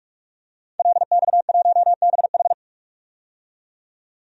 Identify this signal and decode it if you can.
Morse code, heard as a keyed tone, about 46 wpm, 700 Hz. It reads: LX1BH
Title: DXCC-100 Call Signs - 5 Characters - 45wpm